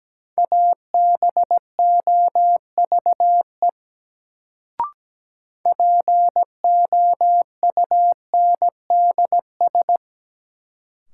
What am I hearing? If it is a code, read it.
Morse code, heard as a keyed tone, about 17 wpm, 700 Hz. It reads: ABOVE POUNDS
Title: Sentences from Top 1000 Words - Review Part 2 - New Word Drill - 17wpm